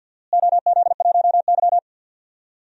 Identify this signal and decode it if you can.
Morse code, heard as a keyed tone, about 52 wpm, 700 Hz. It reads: OZ1Y
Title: DXCC-100 Call Signs - 4 Characters - Repeated 3x - 50wpm